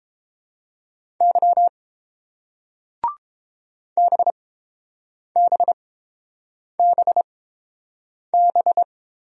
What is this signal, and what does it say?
Morse code, heard as a keyed tone, about 29 wpm, spaced as for 7 wpm, 700 Hz. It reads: Y BBBB